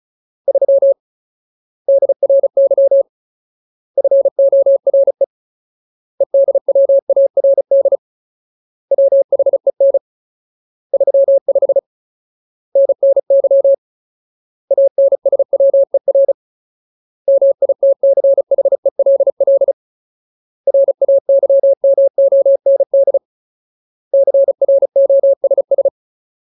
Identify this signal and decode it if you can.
Morse code, heard as a keyed tone, about 34 wpm, 550 Hz. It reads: 3 DRY FORE EDWARD WHEN 35 NNY ANSWER MITCHELL RAYMOND CROSS
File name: Ultimate Word List / 35WPM Ultimate